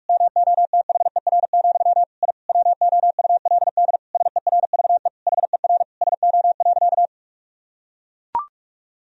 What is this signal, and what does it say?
Morse code, heard as a keyed tone, about 43 wpm, 700 Hz. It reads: MOTHER, I WOULD SERVE HER SO.
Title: Sentences from Top 1000 Words - Review Part 2 - 45wpm